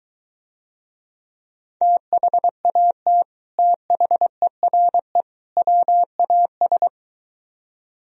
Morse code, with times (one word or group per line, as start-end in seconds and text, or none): 1.8-3.2 THAT
3.6-5.2 THERE
5.6-6.9 WAS